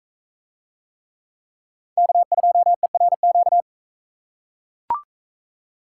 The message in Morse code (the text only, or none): K2ERQ